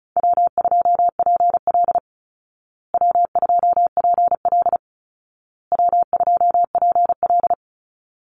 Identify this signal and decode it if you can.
Morse code, heard as a keyed tone, about 34 wpm, 700 Hz. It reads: W2PL W2PL W2PL
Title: Easy Call Signs Repeated 3x - 35wpm